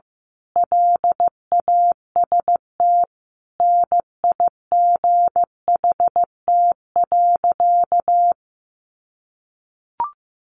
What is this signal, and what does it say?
Morse code, heard as a keyed tone, about 15 wpm, 700 Hz. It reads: LAST NIGHT.
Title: Sentences from Top 300 Words 15wpm